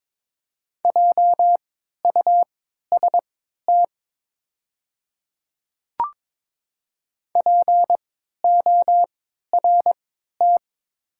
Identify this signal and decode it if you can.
Morse code, heard as a keyed tone, about 22 wpm, spaced as for 12 wpm, 690 Hz. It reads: JUST PORT
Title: 3x Character Spacing - Top 200 Words - 22wpm